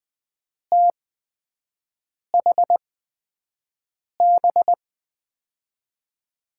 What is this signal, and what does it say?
Morse code, heard as a keyed tone, about 21 wpm, spaced as for 5 wpm, 700 Hz. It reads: THB
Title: USA Call Signs - 6 Characters - 8x Character Spacing - 20wpm